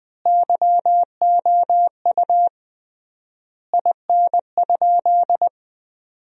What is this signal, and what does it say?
Morse code, heard as a keyed tone, about 21 wpm, 700 Hz. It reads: YOU IN?
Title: 3x Word-Spacing - Sentences from Top 300 Words - 20wpm